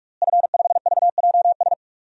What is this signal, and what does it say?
Morse code, heard as a keyed tone, about 46 wpm, 700 Hz. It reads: F5VJS
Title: DXCC-100 Call Signs - 5 Characters - Repeated 2x - 45wpm